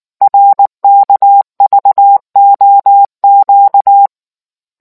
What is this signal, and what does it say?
Morse code, heard as a keyed tone, about 19 wpm, 800 Hz. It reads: RKVOQ